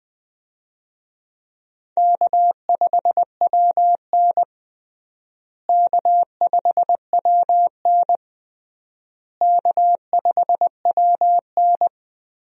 Morse code, heard as a keyed tone, about 21 wpm, 700 Hz. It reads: K5WN K5WN K5WN
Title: USA Call Signs - 4 Characters - Repeated 3x - 20wpm